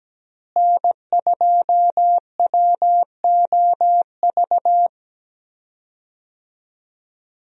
Morse code, heard as a keyed tone, about 17 wpm, 700 Hz. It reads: N2WOV
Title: USA Call Signs - 5 Characters - Repeated 2x - 17wpm